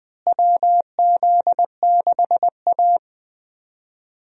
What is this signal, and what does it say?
Morse code, heard as a keyed tone, about 20 wpm, 700 Hz. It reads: WZ6A
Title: USA Call Signs - 4 Characters - Repeated 2x - 20wpm